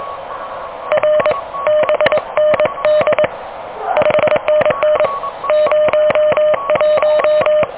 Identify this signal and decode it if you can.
Morse code, heard as a keyed tone, about 22 wpm, 610 Hz, with radio noise. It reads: R BND 5NN 01
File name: OK QRP 2023 / OK QRP 2023